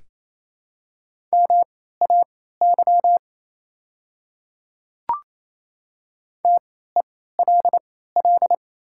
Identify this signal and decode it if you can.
Morse code, heard as a keyed tone, about 27 wpm, spaced as for 16 wpm, 700 Hz. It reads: MAY TELL
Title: 3x Character Spacing - Top 200 Words - 28wpm